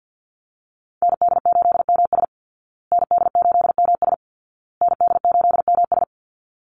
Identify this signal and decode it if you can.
Morse code, heard as a keyed tone, about 50 wpm, 700 Hz. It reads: ND8MS ND8MS ND8MS
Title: Call Signs Repeated 2x - 50wpm